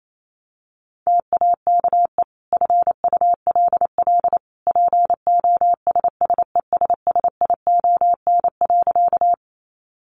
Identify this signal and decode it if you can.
Morse code, heard as a keyed tone, about 28 wpm, 700 Hz. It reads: TAKE FULL POSSESSION.
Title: Sentences from Top 1400 Words 28wpm